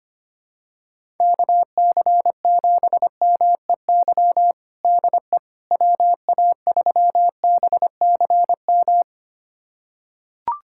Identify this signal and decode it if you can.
Morse code, heard as a keyed tone, about 25 wpm, 700 Hz. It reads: KC7MEY DE WA3BCM
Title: USA Callsign DE Callsign 25wpm